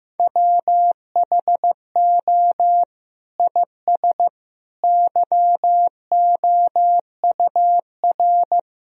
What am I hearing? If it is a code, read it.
Morse code, heard as a keyed tone, about 15 wpm, 700 Hz. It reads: WHO IS YOUR
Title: Sentences from Top 1000 Words - Review Part 4 - 15wpm